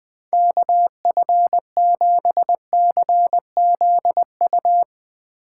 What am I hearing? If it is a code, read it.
Morse code, heard as a keyed tone, about 20 wpm, 700 Hz. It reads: KF7CZU